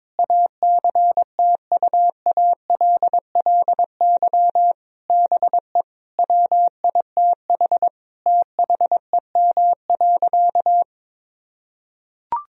Morse code, heard as a keyed tone, about 22 wpm, 700 Hz. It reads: ACTUALLY BE WITH THEM.
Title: Sentences from Top 1000 Words - Review Part 4 - 22wpm